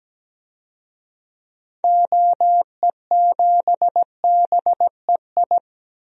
Morse code, heard as a keyed tone, about 17 wpm, 700 Hz. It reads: OE7BEI
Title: DXCC-100 Call Signs - 6 Characters - Repeated 2x - 17wpm